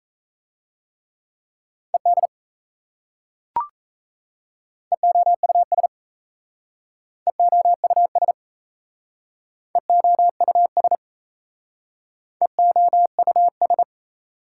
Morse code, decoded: ED EOUS EOUS EOUS EOUS

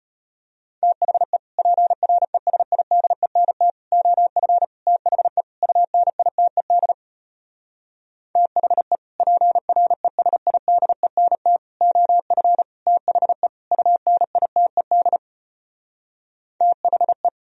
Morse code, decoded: THE PRESIDENT OF THE UNITED THE PRESIDENT OF THE UNITED THE

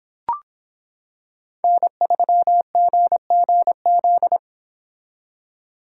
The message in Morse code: N3GGZ

